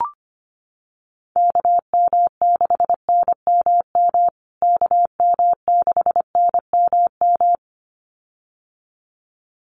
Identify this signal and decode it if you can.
Morse code, heard as a keyed tone, about 25 wpm, 700 Hz. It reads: KM6NMM KM6NMM
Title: USA Call Signs Repeated 2x 25wpm